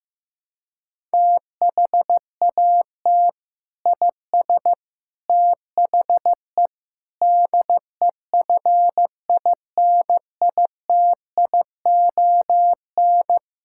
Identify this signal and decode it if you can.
Morse code, heard as a keyed tone, about 15 wpm, 700 Hz. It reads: THAT IS THE DEFINITION
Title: Sentences from Top 1000 Words - Review Part 6 - 15wpm